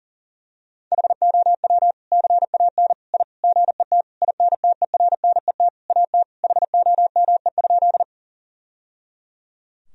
Morse code, heard as a keyed tone, about 37 wpm, 700 Hz. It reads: HOW CAN I GET INTERNET AT HOME?
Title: Sentences from Top 1000 Words - Review Part 4 - 40wpm